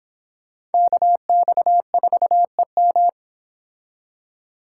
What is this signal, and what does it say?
Morse code, heard as a keyed tone, about 30 wpm, 700 Hz. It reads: KX4EM